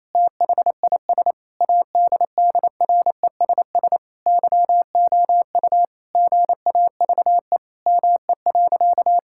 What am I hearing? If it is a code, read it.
Morse code, heard as a keyed tone, about 27 wpm, 700 Hz. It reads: THIS ADDRESS YOU GAVE ME.
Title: Sentences from Top 1000 Words - Review Part 6 - 28wpm